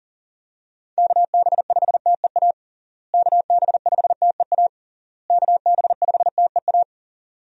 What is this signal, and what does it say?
Morse code, heard as a keyed tone, about 40 wpm, 700 Hz. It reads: KB5TEA KB5TEA KB5TEA
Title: USA Call Signs - 6 Characters - Repeated 3x - 40wpm